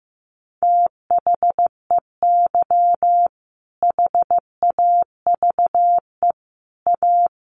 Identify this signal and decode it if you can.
Morse code, heard as a keyed tone, about 15 wpm, 700 Hz. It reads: THEY HAVE A
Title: Sentences from Top 900 Words 15wpm